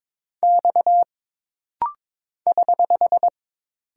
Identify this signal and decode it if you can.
Morse code, heard as a keyed tone, about 22 wpm, 700 Hz. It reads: X <HH>
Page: This rapid-fire timing is geared to push your ICR (Instant Character Recognition) skills.